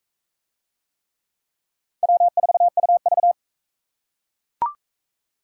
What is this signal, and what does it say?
Morse code, heard as a keyed tone, about 41 wpm, 700 Hz. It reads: W4UV